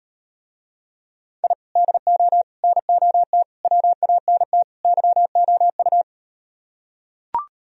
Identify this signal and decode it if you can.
Morse code, heard as a keyed tone, about 38 wpm, 700 Hz. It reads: I DO NOT WANT YOU